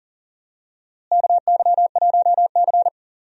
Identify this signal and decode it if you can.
Morse code, heard as a keyed tone, about 40 wpm, 700 Hz. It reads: KY1C
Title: USA Call Signs - 4 Characters - Repeated 2x - 40wpm